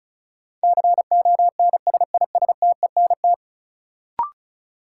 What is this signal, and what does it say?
Morse code, heard as a keyed tone, about 34 wpm, 700 Hz. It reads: CONSISTENT
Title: Sentences from Top 1500 Words - New Word Drill - 35wpm